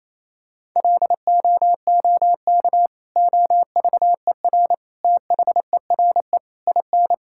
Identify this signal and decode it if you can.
Morse code, heard as a keyed tone, about 28 wpm, 700 Hz. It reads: LOOK OVER THERE IN
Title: Sentences from Top 1600 Words 28wpm